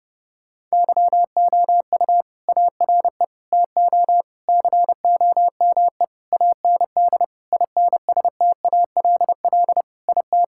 Morse code, heard as a keyed tone, about 32 wpm, 700 Hz. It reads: YOU ARE TO COME AND INSTALL IT
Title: Sentences from Top 1600 Words 30wpm